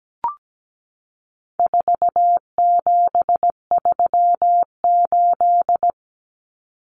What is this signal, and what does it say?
Morse code, heard as a keyed tone, about 17 wpm, 700 Hz. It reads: 4738
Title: Numbers - 4-Digit 17wpm